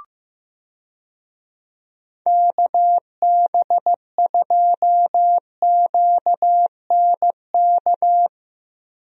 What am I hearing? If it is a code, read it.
Morse code, heard as a keyed tone, about 15 wpm, 700 Hz. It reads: KB2QNK
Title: USA Call Signs - 6 Characters - Repeated 3x - 15wpm